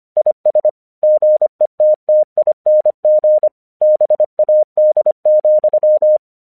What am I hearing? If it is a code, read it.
Morse code, heard as a keyed tone, about 25 wpm, 610 Hz. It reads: IS GETTING BAD,
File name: 1 English 25wpm / AQC02